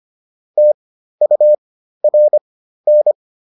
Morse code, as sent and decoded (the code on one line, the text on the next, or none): - ..- .-. -.
TURN